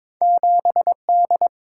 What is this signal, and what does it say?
Morse code, heard as a keyed tone, about 23 wpm, 700 Hz. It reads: 7D
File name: practice / 22wpm / Two Letter-Number Combinations - 22wpm